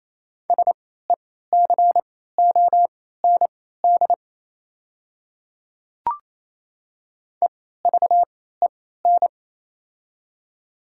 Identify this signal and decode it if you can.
Morse code, heard as a keyed tone, about 28 wpm, spaced as for 16 wpm, 690 Hz. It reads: SECOND EVEN
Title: 3x Character Spacing - Top 300 Words - 28wpm